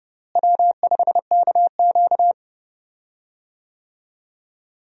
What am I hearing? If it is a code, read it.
Morse code, heard as a keyed tone, about 30 wpm, 700 Hz. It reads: W5KQ
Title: USA Call Signs - 4 Characters - Repeated 3x - 30wpm